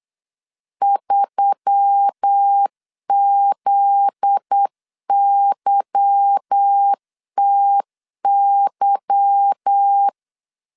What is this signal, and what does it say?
Morse code, heard as a keyed tone, about 8 wpm, 800 Hz. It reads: 3ZYTY
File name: LEZIONI / 10 SIGNS / 10D COMA 8wpm 3749